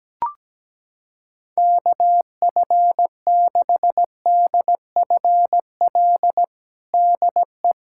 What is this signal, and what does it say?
Morse code, heard as a keyed tone, about 17 wpm, 700 Hz. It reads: KF6DFL DE
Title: USA Callsign DE Callsign - 6 Characters - 17wpm